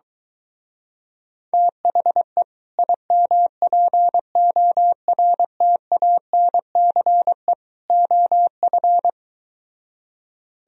Morse code, heard as a keyed tone, about 23 wpm, 700 Hz. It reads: THE IMPORTANCE OF